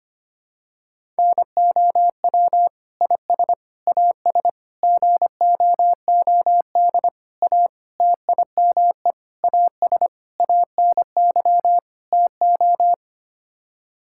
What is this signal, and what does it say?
Morse code, heard as a keyed tone, about 25 wpm, 700 Hz. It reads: NOW IS AS GOOD A TIME AS ANY TO
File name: Sets of 9 Words - 25wpm